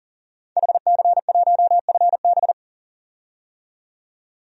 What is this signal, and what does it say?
Morse code, heard as a keyed tone, about 40 wpm, 700 Hz. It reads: HC1FB
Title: DXCC-100 Call Signs - 5 Characters - 40wpm